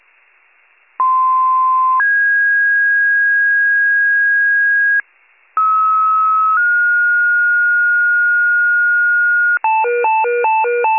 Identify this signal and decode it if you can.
Two-tone page, heard as a one-second tone, then a three-second tone, about 1040 then 1650 Hz, about 1290 then 1430 Hz, with noise. It is heard from 1.0 to 9.6 s.